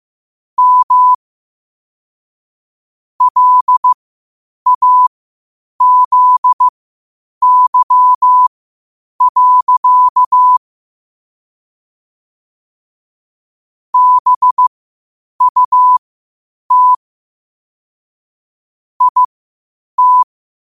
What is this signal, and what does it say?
Morse code, heard as a keyed tone, about 15 wpm, spaced as for 8 wpm, 1000 Hz. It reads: M LAZY. BUT IT
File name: 8wpm